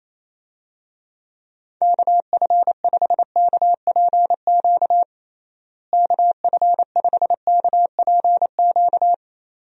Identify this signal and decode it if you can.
Morse code, heard as a keyed tone, about 28 wpm, 700 Hz. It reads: KF5KPQ KF5KPQ
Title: USA Call Signs - 6 Characters - Repeated 2x - 28wpm